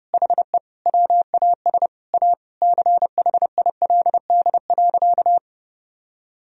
Morse code, heard as a keyed tone, about 32 wpm, 700 Hz. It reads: HE WAS A CHILD.
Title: Sentences from Top 1000 Words - Review Part 3 - 30wpm